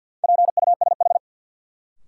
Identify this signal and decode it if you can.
Morse code, heard as a keyed tone, about 52 wpm, 700 Hz. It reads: PUSH
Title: Sentences from Top 1300 Words - New Word Drill - 50wpm